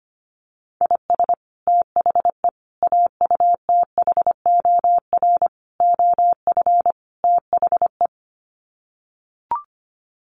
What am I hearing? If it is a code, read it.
Morse code, heard as a keyed tone, about 25 wpm, 700 Hz. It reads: IS THE AUTHOR OF THE
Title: Sets of 5 Words Encore 25wpm